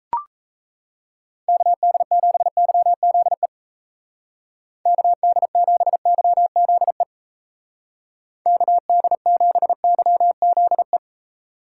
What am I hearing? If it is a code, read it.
Morse code, heard as a keyed tone, about 34 wpm, 700 Hz. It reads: KD7YZE KD7YZE KD7YZE
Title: Speed Racing - USA Call Signs 28wpm